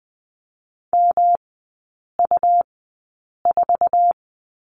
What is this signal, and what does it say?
Morse code, heard as a keyed tone, about 20 wpm, spaced as for 8 wpm, 700 Hz. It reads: MU4
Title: 127 Lesson 21e - Number 6 - Three Characters 20wpm